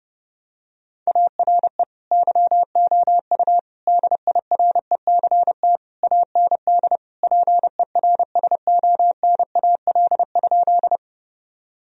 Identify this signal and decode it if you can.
Morse code, heard as a keyed tone, about 32 wpm, 700 Hz. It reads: ARE YOU DIRECT AND PERSONAL?
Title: Sentences from Top 1000 Words - Review Part 4 - 30wpm